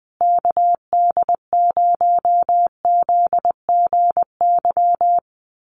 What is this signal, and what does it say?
Morse code, heard as a keyed tone, about 20 wpm, 700 Hz. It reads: KD0ZGY
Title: USA Call Signs Repeated 3x 20wpm